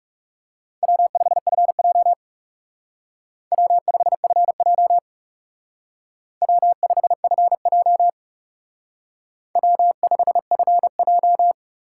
Morse code, heard as a keyed tone, about 34 wpm, 700 Hz. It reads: W5FJ W5FJ W5FJ W5FJ
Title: Speed Racing - USA 1x2 Call Signs 30wpm